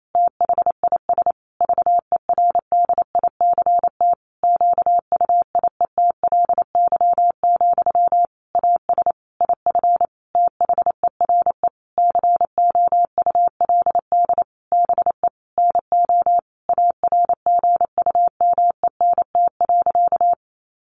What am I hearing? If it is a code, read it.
Morse code, heard as a keyed tone, about 28 wpm, 700 Hz. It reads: THIS VERDICT QUIETLY, AS IF THERE COULD BE NO ARGUMENT.
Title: A Sound Of Thunder 28wpm